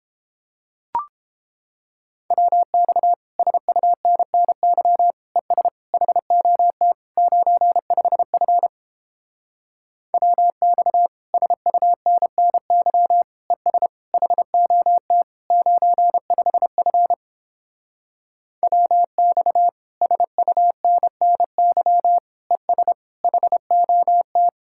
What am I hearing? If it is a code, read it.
Morse code, heard as a keyed tone, about 29 wpm, 700 Hz. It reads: WX SUNNY ES HOT 95F WX SUNNY ES HOT 95F WX SUNNY ES HOT